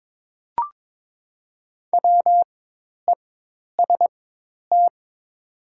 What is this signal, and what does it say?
Morse code, heard as a keyed tone, about 22 wpm, spaced as for 10 wpm, 700 Hz. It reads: WEST
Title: 4x Character Spacing - Top 500 Words - 22wpm